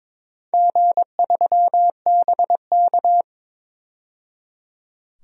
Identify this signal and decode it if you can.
Morse code, heard as a keyed tone, about 22 wpm, 700 Hz. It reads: G3BK
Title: DXCC-100 Call Signs - 4 Characters - 22wpm